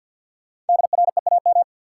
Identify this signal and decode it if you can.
Morse code, heard as a keyed tone, about 53 wpm, 700 Hz. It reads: DREAM
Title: Sentences from Top 1500 Words - New Word Drill Alt - 50wpm